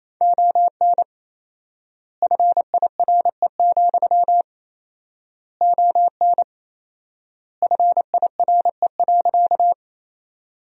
Morse code, heard as a keyed tone, about 28 wpm, 700 Hz. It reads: ON FIRE, ON FIRE.
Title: 4x Word-Spacing - Sentences from Top 400 Words - 28wpm